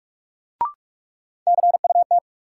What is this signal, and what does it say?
Morse code, heard as a keyed tone, about 42 wpm, 700 Hz. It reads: CUT
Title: Sentences from Top 300 Words - New Word Drill - 45wpm